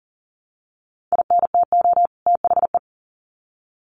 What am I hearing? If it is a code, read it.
Morse code, heard as a keyed tone, about 40 wpm, 700 Hz. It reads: INTO THE
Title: Sets of 2 Words 40wpm